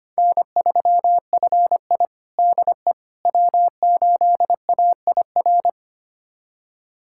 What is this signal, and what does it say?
Morse code, heard as a keyed tone, about 25 wpm, 700 Hz. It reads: N3FI DE W8AIR
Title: USA Callsign DE Callsign 25wpm